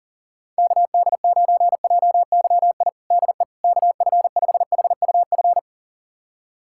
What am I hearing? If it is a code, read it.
Morse code, heard as a keyed tone, about 37 wpm, 700 Hz. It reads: KD9JYI DE KF5HUF
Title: USA Callsign DE Callsign 40wpm